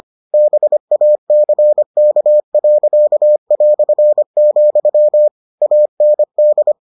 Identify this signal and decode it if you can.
Morse code, heard as a keyed tone, about 25 wpm, 600 Hz. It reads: BACK.", AND